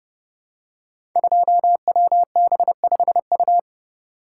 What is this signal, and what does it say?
Morse code, heard as a keyed tone, about 30 wpm, 700 Hz. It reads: 2WB5U
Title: Code Groups - 5 Letter:Numbers - 30wpm